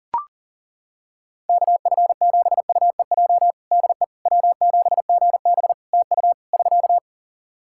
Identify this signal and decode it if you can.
Morse code, heard as a keyed tone, about 37 wpm, 700 Hz. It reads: KF7UEJ DE W7GB TU <SK>
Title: Practice QSO - Exchange 4 - Ending - 40wpm